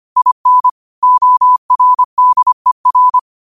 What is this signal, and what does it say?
Morse code, heard as a keyed tone, about 25 wpm, 1000 Hz. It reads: IN ORDER
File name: qotdfiles / 25wpm